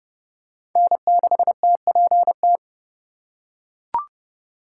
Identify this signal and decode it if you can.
Morse code, heard as a keyed tone, about 30 wpm, 700 Hz. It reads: N6TPT
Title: USA Call Signs - 8x Character Spacing - 30wpm